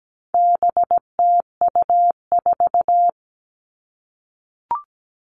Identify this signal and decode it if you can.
Morse code, heard as a keyed tone, about 17 wpm, 700 Hz. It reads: BTU4
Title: All Cut Numbers - 4-Digit 17wpm